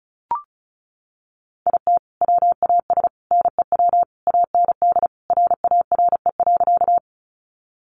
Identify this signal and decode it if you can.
Morse code, heard as a keyed tone, about 34 wpm, 700 Hz. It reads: IT WAS NEW AND RARE.
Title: Sentences from Top 1100 Words 35wpm